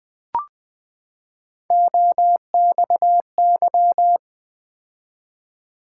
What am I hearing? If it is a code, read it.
Morse code, heard as a keyed tone, about 20 wpm, 700 Hz. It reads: OXY
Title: Build A Word - 3 Letters At A Time - 6-Letter Words - 20wpm